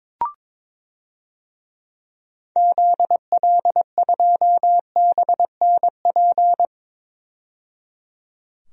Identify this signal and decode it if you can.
Morse code, heard as a keyed tone, about 22 wpm, 700 Hz. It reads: ZL2BNP
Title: DXCC-100 Call Signs - 6 Characters - 22wpm